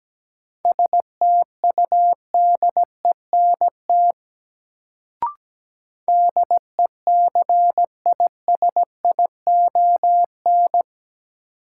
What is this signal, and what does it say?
Morse code, heard as a keyed tone, about 17 wpm, 700 Hz. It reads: STUDENT DECISION
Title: Sentences from Top 1000 Words - New Word Drill - 17wpm